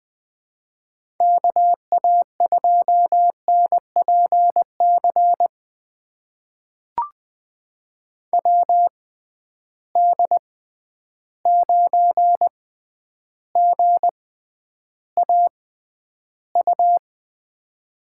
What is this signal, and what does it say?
Morse code, heard as a keyed tone, about 20 wpm, spaced as for 10 wpm, 700 Hz. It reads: KA2NPC WD9GAU
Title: USA Call Signs - 6 Characters - 6x Character Spacing - 20wpm